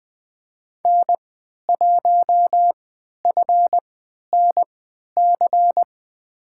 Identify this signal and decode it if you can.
Morse code, heard as a keyed tone, about 20 wpm, spaced as for 11 wpm, 700 Hz. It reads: N1FNC
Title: USA Call Signs - 5 Characters - 3x Character Spacing - 20wpm